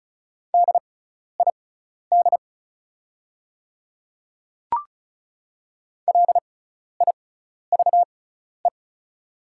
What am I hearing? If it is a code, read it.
Morse code, heard as a keyed tone, about 36 wpm, spaced as for 12 wpm, 700 Hz. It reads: DID LIVE
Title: 6x Character Spacing - Top 300 Words - 35wpm